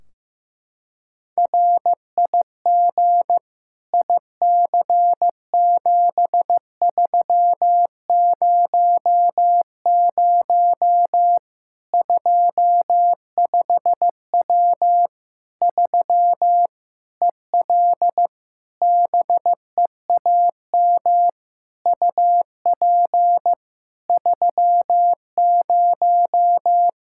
Morse code, heard as a keyed tone, about 15 wpm, 700 Hz. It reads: RIG IC7300 25W 3 EL BEAM UP 30
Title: Practice QSO - Exchange 2 - Info - 15wpm